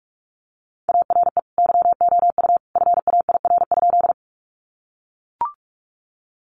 Morse code, heard as a keyed tone, about 44 wpm, 700 Hz. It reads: ARE YOU FAIR?
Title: Sentences from Top 1300 Words 45wpm